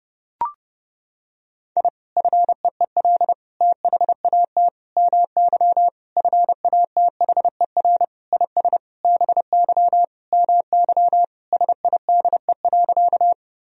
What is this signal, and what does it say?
Morse code, heard as a keyed tone, about 32 wpm, 700 Hz. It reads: I FEEL THAT MY FATHER IS BY MY SIDE.
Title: Sentences from Top 1000 Words - Review Part 1 - 30wpm